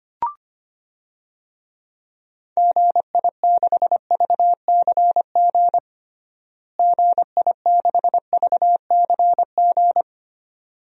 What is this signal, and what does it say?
Morse code, heard as a keyed tone, about 25 wpm, 700 Hz. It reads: GI6VCG GI6VCG
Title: DXCC-100 Call Signs - 6 Characters - Repeated 2x - 25wpm